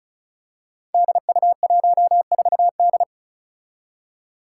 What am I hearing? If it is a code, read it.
Morse code, heard as a keyed tone, about 34 wpm, 700 Hz. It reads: DU14D
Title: Five Challenging Letter-Number Combinations 35wpm